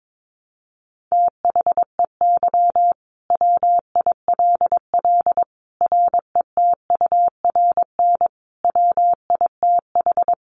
Morse code, heard as a keyed tone, about 22 wpm, 700 Hz. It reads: THEY WILL RETURN WITH
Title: Sentences from Top 800 Words 22wpm